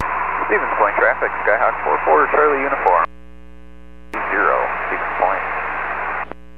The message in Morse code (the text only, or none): none